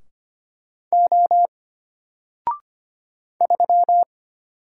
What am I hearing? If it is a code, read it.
Morse code, heard as a keyed tone, about 25 wpm, 700 Hz. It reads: O 3
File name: Single-Letter-Number-25wpm